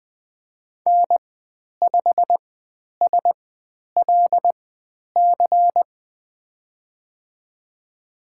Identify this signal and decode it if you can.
Morse code, heard as a keyed tone, about 20 wpm, spaced as for 10 wpm, 700 Hz. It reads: N5SLC
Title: 102 Lesson 17g - Letter u - Call Signs 20wpm